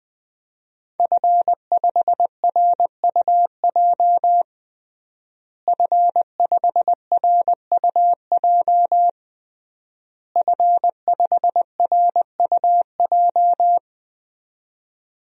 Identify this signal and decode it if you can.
Morse code, heard as a keyed tone, about 20 wpm, 700 Hz. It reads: F5RUJ F5RUJ F5RUJ
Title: DXCC-100 Call Signs - 5 Characters - Repeated 3x - 20wpm